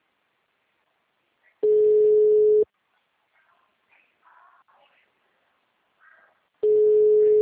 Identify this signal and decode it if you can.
Ringback tone (Europe, 425 Hz).
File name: out- -2038- - - .4492